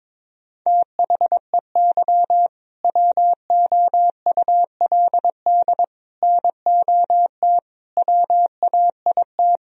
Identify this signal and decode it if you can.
Morse code, heard as a keyed tone, about 22 wpm, 700 Hz. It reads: THEY WOULD NOT WAIT
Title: Sentences from Top 1000 Words - Review Part 3 - 22wpm